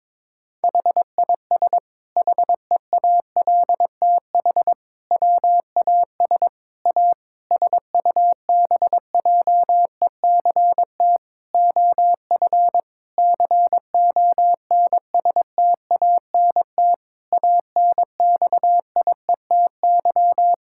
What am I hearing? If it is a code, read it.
Morse code, heard as a keyed tone, about 22 wpm, 700 Hz. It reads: HIS HEALTH WAS A SUBJECT OF CONSTANT ANXIETY
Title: Sentences from Top 1600 Words 22wpm